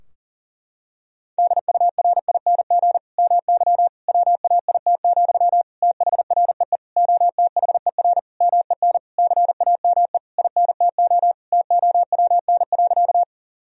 Morse code, heard as a keyed tone, about 40 wpm, 700 Hz. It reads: DURING MY WAIT, THREE OTHER MEN CAME INTO TOWN.
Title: Sentences from Top 1000 Words - Review Part 2 - 40wpm